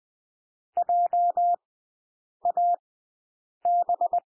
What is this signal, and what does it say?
Morse code, heard as a keyed tone, about 20 wpm, spaced as for 8 wpm, 700 Hz. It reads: JAB
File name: cwmp3 / W1 SOUND WORDS 20 wpm